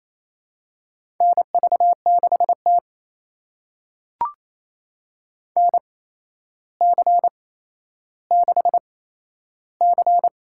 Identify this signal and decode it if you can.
Morse code, heard as a keyed tone, about 28 wpm, spaced as for 12 wpm, 700 Hz. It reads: NV6T NC6C
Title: USA Call Signs - 4 Characters - 8x Character Spacing - 28wpm